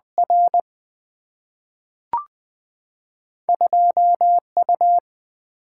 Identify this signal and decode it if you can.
Morse code, heard as a keyed tone, about 21 wpm, 700 Hz. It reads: R 2U
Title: Single-Letter Single-Number 20wpm